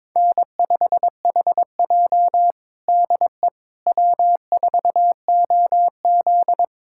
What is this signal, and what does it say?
Morse code, heard as a keyed tone, about 22 wpm, 700 Hz. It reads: N5HJ DE W4OZ
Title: USA Callsign DE Callsign - 4 Characters - 22wpm